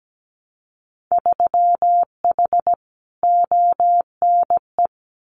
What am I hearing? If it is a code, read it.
Morse code, heard as a keyed tone, about 17 wpm, 700 Hz. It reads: 3H ONE
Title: Winter Field Day Exchanges Encore 17wpm